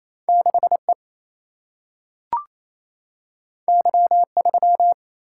Morse code, heard as a keyed tone, about 28 wpm, 700 Hz. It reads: 6E Y3
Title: Single-Letter Single-Number 28wpm